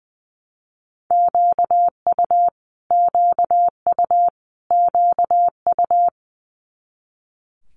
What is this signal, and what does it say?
Morse code, heard as a keyed tone, about 19 wpm, 700 Hz. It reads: QU QU QU
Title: Multi-letter Phonemes Repeated 3x 20wpm